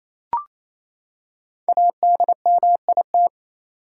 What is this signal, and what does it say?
Morse code, heard as a keyed tone, about 28 wpm, 700 Hz. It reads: ADMIT
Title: Five Letter Words - 28wpm